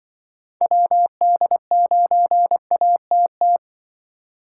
Morse code, heard as a keyed tone, about 25 wpm, 700 Hz. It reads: WD9ATT